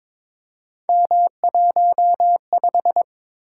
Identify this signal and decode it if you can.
Morse code, heard as a keyed tone, about 22 wpm, 700 Hz. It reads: M15
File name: practice / 22wpm / Three Letter-Number Combinations - 22wpm